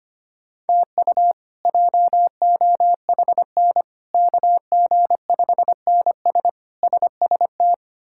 Morse code, heard as a keyed tone, about 25 wpm, 700 Hz. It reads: TU JOHN KG5NS SST
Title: K1USN SST QSO Practice - 25wpm